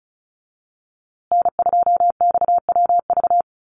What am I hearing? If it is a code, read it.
Morse code, heard as a keyed tone, about 34 wpm, 700 Hz. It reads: N2XWV
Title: USA Call Signs Repeated 3x 35wpm